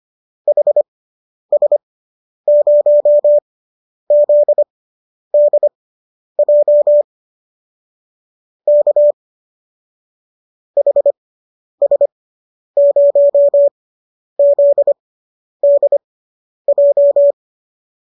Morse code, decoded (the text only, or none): HS0ZDJ K HS0ZDJ